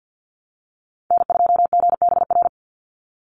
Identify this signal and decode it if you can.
Morse code, heard as a keyed tone, about 49 wpm, 700 Hz. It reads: N2GBR
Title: NA SOTA Chaser Callsigns 50wpm